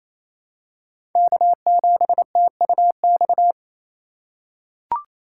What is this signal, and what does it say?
Morse code, heard as a keyed tone, about 28 wpm, 700 Hz. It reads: K7TUX
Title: USA Call Signs - 5 Characters - 2x Character Spacing - 28wpm